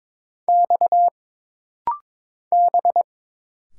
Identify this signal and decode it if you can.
Morse code, heard as a keyed tone, about 22 wpm, 700 Hz. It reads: X B
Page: This rapid-fire timing is geared to push your ICR (Instant Character Recognition) skills.